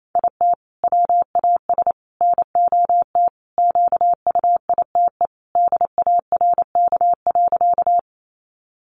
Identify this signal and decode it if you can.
Morse code, heard as a keyed tone, about 28 wpm, 700 Hz. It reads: IT WAS NOT QUITE DARK.
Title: Sentences from Top 700 Words 28wpm